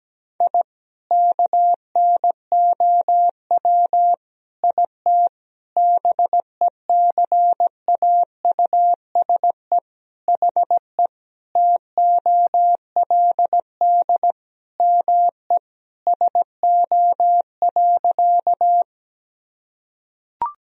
Morse code, heard as a keyed tone, about 17 wpm, 700 Hz. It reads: I KNOW IT BECAUSE HE TOLD ME SO.
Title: Sentences from Top 1000 Words - Review Part 3 - 17wpm